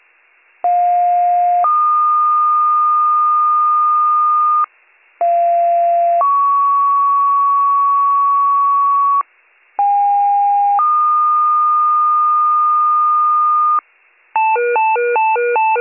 Two-tone page, heard as a one-second tone, then a three-second tone, about 710 then 1220 Hz, about 690 then 1120 Hz, about 810 then 1220 Hz. It is heard from 0.6 to 13.8 s.